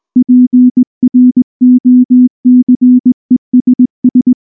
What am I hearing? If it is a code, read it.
Morse code, heard as a keyed tone, about 19 wpm, 250 Hz. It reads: PROCESS
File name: morse / morse-training-20WPM-6